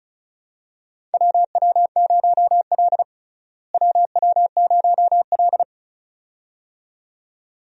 Morse code, heard as a keyed tone, about 34 wpm, 700 Hz. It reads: WW0L WW0L
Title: USA Call Signs - 4 Characters - Repeated 2x - 35wpm